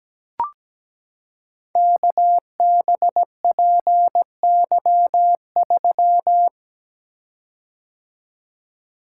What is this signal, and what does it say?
Morse code, heard as a keyed tone, about 17 wpm, 700 Hz. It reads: KBPY3
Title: Code Groups - 5 Letter:Numbers - 17wpm